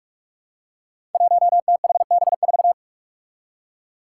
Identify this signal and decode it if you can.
Morse code, heard as a keyed tone, about 46 wpm, 700 Hz. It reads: 1THB4
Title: Five Challenging Letter-Number Combinations 45wpm